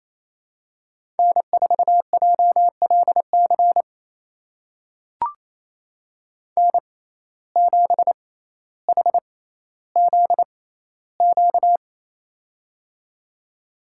Morse code, heard as a keyed tone, about 28 wpm, spaced as for 14 wpm, 700 Hz. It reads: N4JLC N7HZQ
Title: USA Call Signs - 5 Characters - 6x Character Spacing - 28wpm